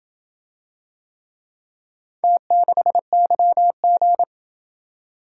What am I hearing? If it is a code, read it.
Morse code, heard as a keyed tone, about 28 wpm, 700 Hz. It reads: T6YG